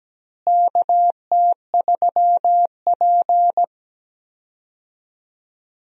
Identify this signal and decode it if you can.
Morse code, heard as a keyed tone, about 17 wpm, 700 Hz. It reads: KT3P
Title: USA Call Signs - 4 Characters - Repeated 2x - 17wpm